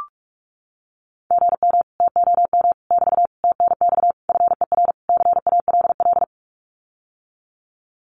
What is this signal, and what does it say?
Morse code, heard as a keyed tone, about 44 wpm, 700 Hz. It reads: GM TOM <BT> TNX FER CALL
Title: Common QSO Phrases Encore 45wpm